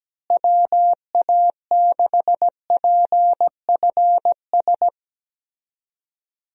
Morse code, heard as a keyed tone, about 17 wpm, 700 Hz. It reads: WA6PFS
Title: USA Call Signs - 6 Characters - Repeated 2x - 17wpm